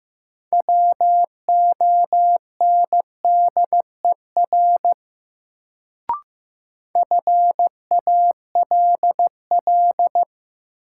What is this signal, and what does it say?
Morse code, heard as a keyed tone, about 15 wpm, 700 Hz. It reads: WONDER FALL
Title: Sentences from Top 500 Words - New Word Drill - 15wpm